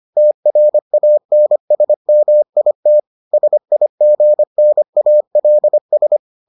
Morse code, heard as a keyed tone, about 25 wpm, 600 Hz. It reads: TRANSMIT SIGNALS